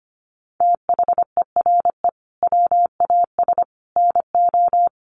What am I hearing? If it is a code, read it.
Morse code, heard as a keyed tone, about 25 wpm, 700 Hz. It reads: THERE WAS NO
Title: Sets of 3 Words 25wpm